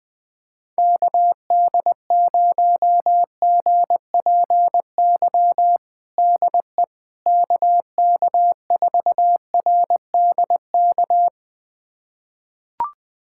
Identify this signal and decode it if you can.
Morse code, heard as a keyed tone, about 20 wpm, 700 Hz. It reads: KD0GPY DE KK4RDK
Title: USA Callsign DE Callsign 20wpm